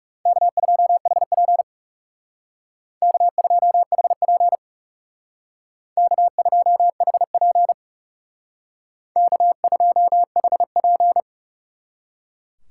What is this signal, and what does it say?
Morse code, heard as a keyed tone, about 38 wpm, 700 Hz. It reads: K2HP K2HP K2HP K2HP